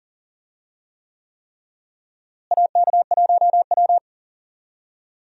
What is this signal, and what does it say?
Morse code, heard as a keyed tone, about 40 wpm, 700 Hz. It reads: AK1W
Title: USA Call Signs - 4 Characters - Repeated 3x - 40wpm